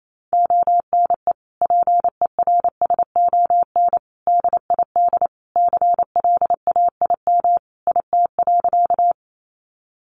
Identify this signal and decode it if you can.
Morse code, heard as a keyed tone, about 28 wpm, 700 Hz. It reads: ONE PERSON DID CLAIM IT.
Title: Sentences from Top 1500 Words 28wpm